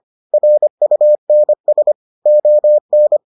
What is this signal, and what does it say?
Morse code, heard as a keyed tone, about 25 wpm, 600 Hz. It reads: RUNS ON